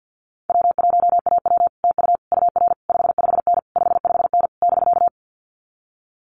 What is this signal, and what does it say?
Morse code, heard as a keyed tone, about 50 wpm, 700 Hz. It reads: W1AW TU UR 55N 55N <BK>
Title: Common QSO Phrases Encore 50wpm